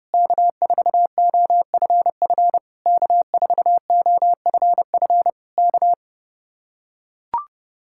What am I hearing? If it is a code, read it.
Morse code, heard as a keyed tone, about 30 wpm, 700 Hz. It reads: K4OFF K4OFF K